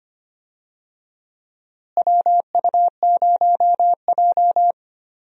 Morse code, heard as a keyed tone, about 25 wpm, 700 Hz. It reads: WU0J
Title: USA Call Signs - 4 Characters - Repeated 3x - 25wpm